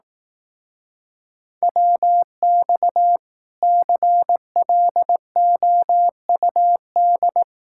Morse code, heard as a keyed tone, about 18 wpm, 700 Hz. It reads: WX CLOUD
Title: Speed-Racing Common QSO Phrases Encore - 15wpm